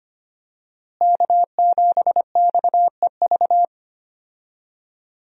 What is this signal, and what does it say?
Morse code, heard as a keyed tone, about 25 wpm, 700 Hz. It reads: K7XEV
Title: USA Call Signs - 5 Characters - Repeated 2x - 25wpm